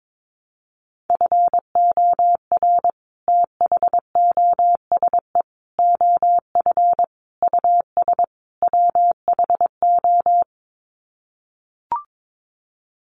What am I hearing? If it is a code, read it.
Morse code, heard as a keyed tone, about 22 wpm, 700 Hz. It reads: FOR THOSE OF US WHO
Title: Sets of 5 Words Encore 22wpm